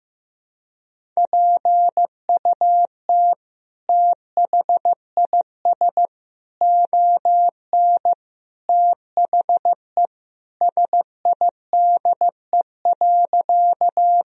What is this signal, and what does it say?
Morse code, heard as a keyed tone, about 15 wpm, 700 Hz. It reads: PUT THIS ON THE SIDE.
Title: Sentences from Top 1000 Words - Review Part 1 - 15wpm